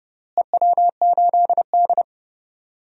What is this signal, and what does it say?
Morse code, heard as a keyed tone, about 30 wpm, 700 Hz. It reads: EW8D
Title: DXCC-100 Call Signs - 4 Characters - Repeated 2x - 30wpm